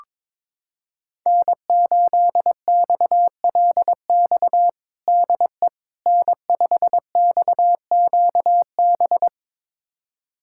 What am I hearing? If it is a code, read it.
Morse code, heard as a keyed tone, about 22 wpm, 700 Hz. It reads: N8XLX DE N5XQB
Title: USA Callsign DE Callsign - 5 Characters - 22wpm